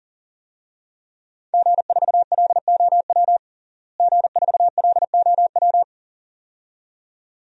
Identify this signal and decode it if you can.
Morse code, heard as a keyed tone, about 37 wpm, 700 Hz. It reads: G4LOW G4LOW
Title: DXCC-100 Call Signs - 5 Characters - Repeated 2x - 40wpm